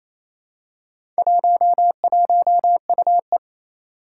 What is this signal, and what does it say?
Morse code, heard as a keyed tone, about 28 wpm, 700 Hz. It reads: 11UE
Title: Four Challenging Letter-Number Combinations 28wpm